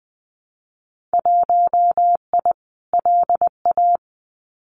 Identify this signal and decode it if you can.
Morse code, heard as a keyed tone, about 20 wpm, 700 Hz. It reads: 1I LA
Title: Winter Field Day Exchanges Encore 20wpm